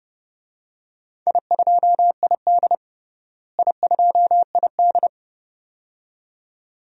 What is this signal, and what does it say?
Morse code, heard as a keyed tone, about 29 wpm, 700 Hz. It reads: I2ID I2ID
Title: DXCC-100 Call Signs - 4 Characters - Repeated 2x - 30wpm